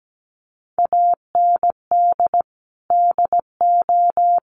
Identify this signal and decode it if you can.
Morse code, heard as a keyed tone, about 17 wpm, 700 Hz. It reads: AND DO
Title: Sentences from Top 100 Words 17wpm